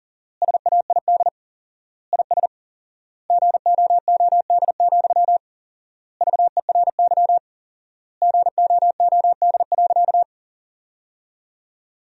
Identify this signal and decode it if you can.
Morse code, heard as a keyed tone, about 37 wpm, 700 Hz. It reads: SAID IS GOOD, VERY GOOD.
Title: 4x Word-Spacing - Sentences from Top 200 Words - 40wpm